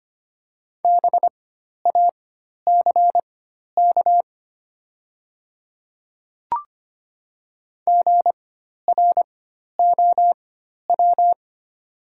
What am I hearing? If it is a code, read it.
Morse code, heard as a keyed tone, about 25 wpm, spaced as for 12 wpm, 700 Hz. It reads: BACK GROW
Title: 4x Character Spacing - Top 300 Words - 25wpm